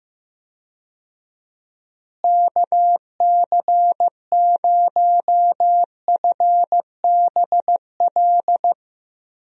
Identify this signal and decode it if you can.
Morse code, heard as a keyed tone, about 15 wpm, 700 Hz. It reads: KC0FBL
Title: USA Call Signs - 6 Characters - Repeated 2x - 15wpm